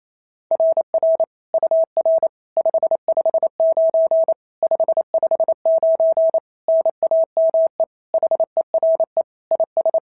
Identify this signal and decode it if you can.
Morse code, heard as a keyed tone, about 28 wpm, 690 Hz. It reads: RR UR 559 559 NAME HERE IS
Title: SOTA QSO Practice 28wpm